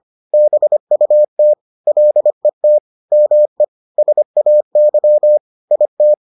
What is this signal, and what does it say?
Morse code, heard as a keyed tone, about 25 wpm, 600 Hz. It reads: BUT LET ME SAY IT